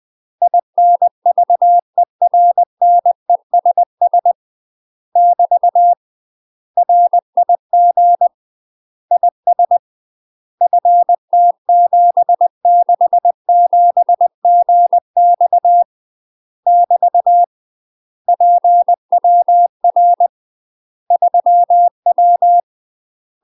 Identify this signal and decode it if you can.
Morse code, heard as a keyed tone, about 20 wpm, 700 Hz. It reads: INVERNESS <BT> RIG IS FT767GX <BT> PWR 3W
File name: UK MASTER QSOs TRAINING / 20WPM / UK MASTER QSOs 0023